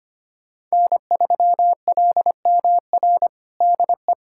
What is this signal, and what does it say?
Morse code, heard as a keyed tone, about 25 wpm, 700 Hz. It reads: N3LMR DE
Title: USA Callsign DE Callsign 25wpm